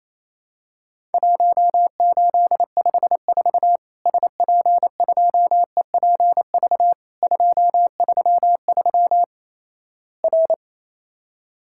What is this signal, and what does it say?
Morse code, heard as a keyed tone, about 28 wpm, 700 Hz. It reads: 1854 SP2EPV 233 R
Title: Worked All Europe DX Contest - QTC Exchange - 28wpm